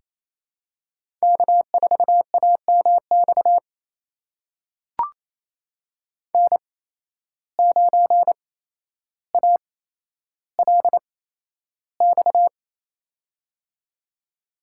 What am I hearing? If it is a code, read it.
Morse code, heard as a keyed tone, about 27 wpm, spaced as for 12 wpm, 700 Hz. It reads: K4AMX N9ALX
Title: USA Call Signs - 5 Characters - 8x Character Spacing - 28wpm